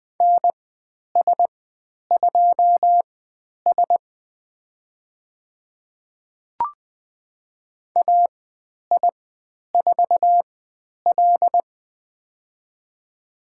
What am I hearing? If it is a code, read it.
Morse code, heard as a keyed tone, about 20 wpm, spaced as for 10 wpm, 700 Hz. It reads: NS2S AI4L
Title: 095 Lesson 16g - Letter c - Call Signs 20wpm